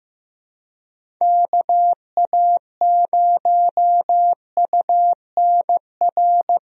Morse code, heard as a keyed tone, about 15 wpm, 700 Hz. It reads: KA0UNR
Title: USA Call Signs - 6 Characters - Repeated 2x - 15wpm